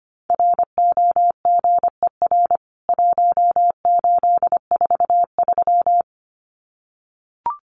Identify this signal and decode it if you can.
Morse code, heard as a keyed tone, about 25 wpm, 700 Hz. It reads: ROGER 1843
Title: CWT Exchange 25wpm